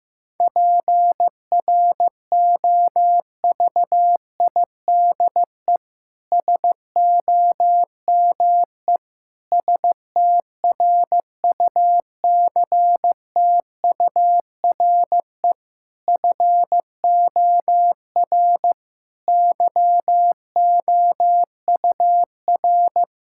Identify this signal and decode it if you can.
Morse code, heard as a keyed tone, about 15 wpm, 700 Hz. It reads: PROVIDE SOME STRUCTURE FOR YOUR
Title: Sentences from Top 1000 Words - Review Part 6 - 15wpm